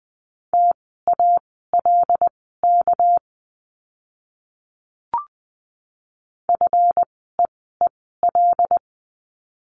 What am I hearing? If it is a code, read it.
Morse code, heard as a keyed tone, about 20 wpm, spaced as for 15 wpm, 700 Hz. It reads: TALK FEEL
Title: 206 Lesson 33f - Letter j - Words 20wpm